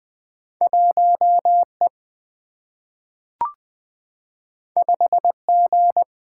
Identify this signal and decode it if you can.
Morse code, heard as a keyed tone, about 20 wpm, 700 Hz. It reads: 1 E 5 G
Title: Single-Letter Single-Number 20wpm